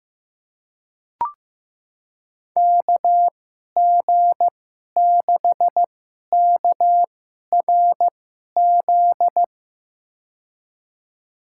Morse code, heard as a keyed tone, about 15 wpm, spaced as for 11 wpm, 700 Hz. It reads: KG6KRZ
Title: USA Call Signs - 6 Characters - 2x Character Spacing - 15wpm